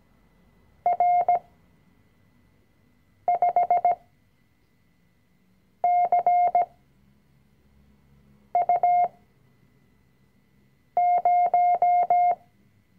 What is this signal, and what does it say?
Morse code, heard as a keyed tone, about 17 wpm, spaced as for 4 wpm, 710 Hz. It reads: R5CU0
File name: CW-Online-Kurs / Anfänger / Lektion 005